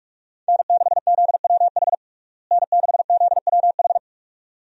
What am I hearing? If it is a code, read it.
Morse code, heard as a keyed tone, about 46 wpm, 700 Hz. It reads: N6ZWH N6ZWH
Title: USA Call Signs - 5 Characters - Repeated 2x - 45wpm